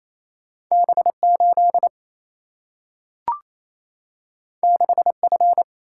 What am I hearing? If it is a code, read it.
Morse code, heard as a keyed tone, about 28 wpm, 700 Hz. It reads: B8 6F
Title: Single-Letter Single-Number 28wpm